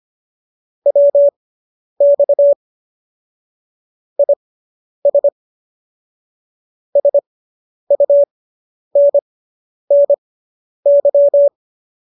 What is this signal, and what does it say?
Morse code, heard as a keyed tone, about 25 wpm, spaced as for 10 wpm, 570 Hz. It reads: WX IS SUNNY